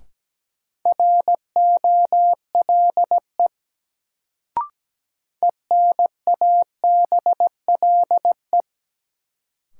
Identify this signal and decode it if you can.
Morse code, heard as a keyed tone, about 17 wpm, 700 Hz. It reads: ROLE ENABLE
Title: Sentences from Top 1500 Words - New Word Drill Alt - 17wpm